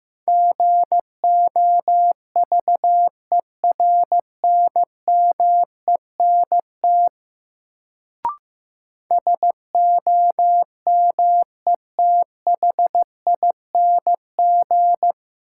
Morse code, heard as a keyed tone, about 15 wpm, 700 Hz. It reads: GOVERNMENT SOMETHING
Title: Sentences from Top 600 Words - New Word Drill - 15wpm